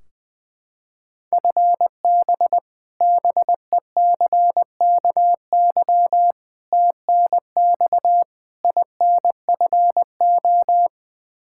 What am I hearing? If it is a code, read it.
Morse code, heard as a keyed tone, about 20 wpm, 700 Hz. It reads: FB BECKY TNX INFO
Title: Practice QSO - Exchange 3 - More Info - 20wpm